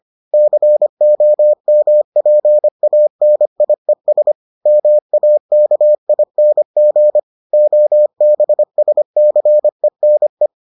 Morse code, heard as a keyed tone, about 25 wpm, 600 Hz. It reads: COMPANIES MAKING OBSCENE